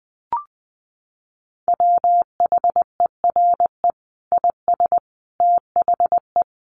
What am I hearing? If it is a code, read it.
Morse code, heard as a keyed tone, about 20 wpm, 700 Hz. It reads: WHERE IS THE
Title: Sentences from Top 1100 Words 20wpm